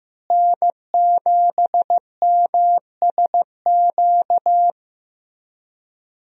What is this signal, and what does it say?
Morse code, heard as a keyed tone, about 15 wpm, 700 Hz. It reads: N7MSQ
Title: USA Call Signs - 5 Characters - Repeated 3x - 15wpm